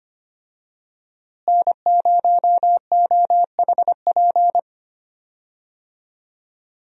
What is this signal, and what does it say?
Morse code, heard as a keyed tone, about 25 wpm, 700 Hz. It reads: N0OHP
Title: USA Call Signs - 5 Characters - Repeated 3x - 25wpm